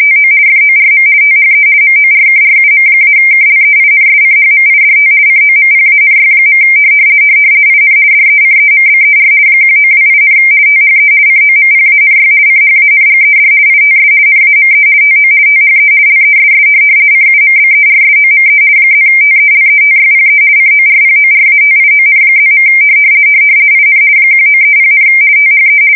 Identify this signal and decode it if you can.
RTTY, about 50 baud. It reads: TO FORM THE MORKRUM-KLEINSCHMIDT COMPANY. IN DECEMBER 1928, THE COMPANY CHANGED ITS NAME TO THE LESS CUMBERSOME +TELETYPE CORPORATION+. IN 1930, THE
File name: fsk170-50B / teletype-corp